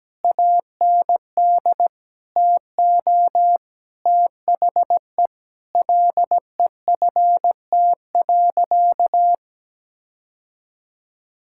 Morse code, heard as a keyed tone, about 17 wpm, 700 Hz. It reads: AND TO THE LEFT.
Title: Sentences from Top 1600 Words 17wpm